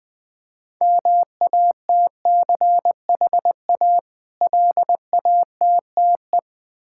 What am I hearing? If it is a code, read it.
Morse code, heard as a keyed tone, about 20 wpm, 700 Hz. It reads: MATCHA LATTE
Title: Difficult Words with Hints - 20wpm